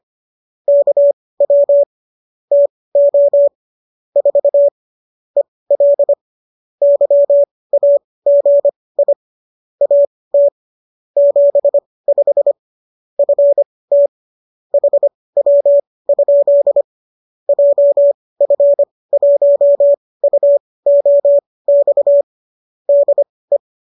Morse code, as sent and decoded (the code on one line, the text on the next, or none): -.- .-- / - --- / ....- / . .-.. / -.-- .- --. .. / .- - / --... ..... / ..-. - / .... .-- ..--.. / .--- ..-. .---- ..- --- -..- / -.. .
KW TO 4 EL YAGI AT 75 FT HW? JF1UOX DE